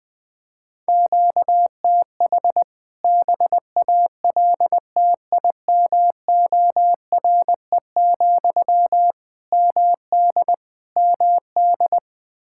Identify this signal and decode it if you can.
Morse code, heard as a keyed tone, about 20 wpm, 700 Hz. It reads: QTH BALTIMORE, MD MD
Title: K1USN SST QSO Practice 20wpm